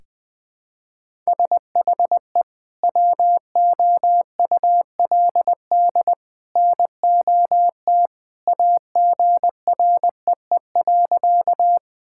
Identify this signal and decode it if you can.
Morse code, heard as a keyed tone, about 20 wpm, 700 Hz. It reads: SHE WOULD NOT AGREE.
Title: Sentences with 2-Word Phrases - Spoken Beginning - 20wpm